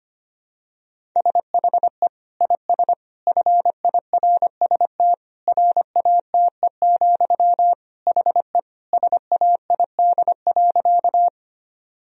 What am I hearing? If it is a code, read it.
Morse code, heard as a keyed tone, about 25 wpm, 700 Hz. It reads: SHE IS FIRST RATE, HE SAID.
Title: Sentences from Top 1000 Words - Review Part 4 - 25wpm